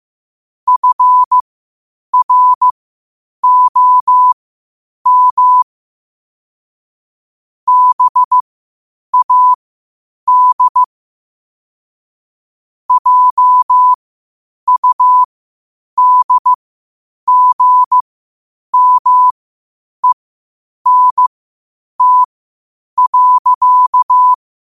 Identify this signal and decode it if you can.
Morse code, heard as a keyed tone, about 15 wpm, spaced as for 8 wpm, 1000 Hz. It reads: FROM BAD JUDGMENT.
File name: qotdfiles / 8wpm